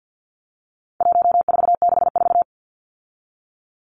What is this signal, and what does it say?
Morse code, heard as a keyed tone, about 50 wpm, 700 Hz. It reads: 1464
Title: Numbers - 4-Digit 50wpm